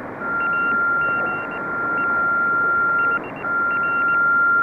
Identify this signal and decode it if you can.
Morse code, heard as a keyed tone, about 19 wpm, spaced as for 15 wpm, 2750 Hz, with noise. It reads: AGE HR
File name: Ham 2 / 2006.03.04.VE1JF.7060.7.khz.lsb.411.pm.est